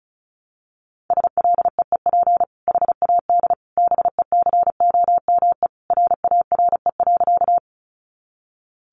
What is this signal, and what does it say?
Morse code, heard as a keyed tone, about 34 wpm, 700 Hz. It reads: SLEEP HAD BECOME RARE.
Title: Sentences from Top 1100 Words 35wpm